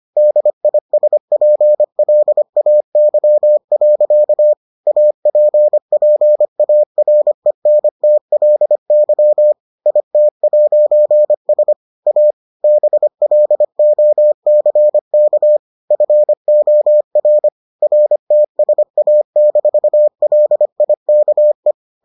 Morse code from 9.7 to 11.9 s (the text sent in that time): IT'S